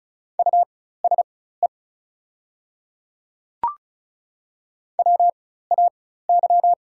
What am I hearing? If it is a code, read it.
Morse code, heard as a keyed tone, about 36 wpm, spaced as for 16 wpm, 700 Hz. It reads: USE WAY
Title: 4x Character Spacing - Top 200 Words - 35wpm